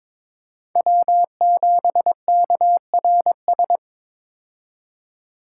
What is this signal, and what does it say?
Morse code, heard as a keyed tone, about 22 wpm, 700 Hz. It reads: W7KRS
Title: K1USN SST QSO Practice - 22wpm